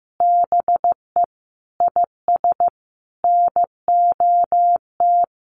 Morse code, heard as a keyed tone, about 15 wpm, 700 Hz. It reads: BE IS NOT
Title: Sentences from Top 1300 Words 15wpm